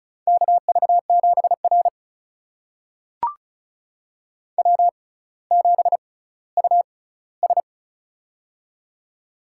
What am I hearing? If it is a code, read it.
Morse code, heard as a keyed tone, about 36 wpm, spaced as for 18 wpm, 700 Hz. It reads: KV7R W7US
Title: USA Call Signs - 4 Characters - 6x Character Spacing - 35wpm